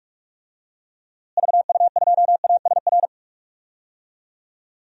Morse code, heard as a keyed tone, about 46 wpm, 700 Hz. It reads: VU2ASR
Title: DXCC-100 Call Signs - 6 Characters - Repeated 2x - 45wpm